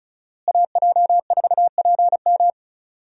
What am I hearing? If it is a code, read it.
Morse code, heard as a keyed tone, about 34 wpm, 700 Hz. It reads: AJ4PM
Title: USA Call Signs - 5 Characters - Repeated 3x - 35wpm